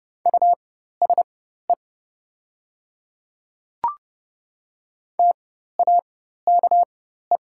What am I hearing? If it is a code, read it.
Morse code, heard as a keyed tone, about 30 wpm, spaced as for 14 wpm, 700 Hz. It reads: USE TAKE
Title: 4x Character Spacing - Top 200 Words - 30wpm